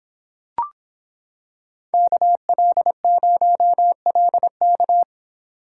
Morse code, heard as a keyed tone, about 25 wpm, 700 Hz. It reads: KL0LK